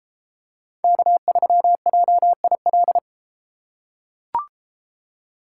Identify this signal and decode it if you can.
Morse code, heard as a keyed tone, about 32 wpm, 700 Hz. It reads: K3JIL